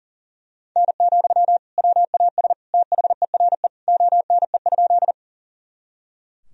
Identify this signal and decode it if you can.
Morse code, heard as a keyed tone, about 40 wpm, 700 Hz. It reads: N, WAS THERE ONE?
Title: Sentences from Top 100 Words and Abbreviations 40wpm